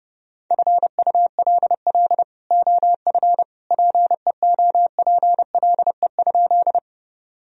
Morse code, heard as a keyed tone, about 30 wpm, 700 Hz. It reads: FULL OF PEOPLE?
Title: Sentences from Top 1000 Words - Review Part 3 - 30wpm